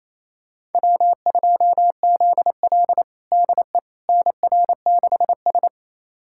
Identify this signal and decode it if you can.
Morse code, heard as a keyed tone, about 28 wpm, 700 Hz. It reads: W2ZL DE NR6S
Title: USA Callsign DE Callsign - 4 Characters - 28wpm